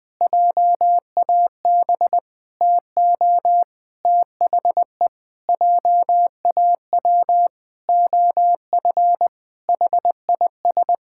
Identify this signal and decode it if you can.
Morse code, heard as a keyed tone, about 20 wpm, 700 Hz. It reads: JAB TO THE JAW OF HIS
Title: Pangram Challenge v2 - 20wpm